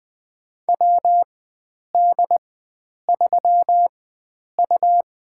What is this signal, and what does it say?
Morse code, heard as a keyed tone, about 20 wpm, spaced as for 9 wpm, 700 Hz. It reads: WD3U
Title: USA Call Signs - 4 Characters - 4x Character Spacing - 20wpm